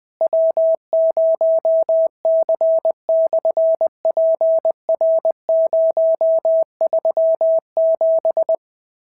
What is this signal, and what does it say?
Morse code, heard as a keyed tone, about 20 wpm, 650 Hz. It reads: W0C/PR037
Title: SOTA QSO Practice 20wpm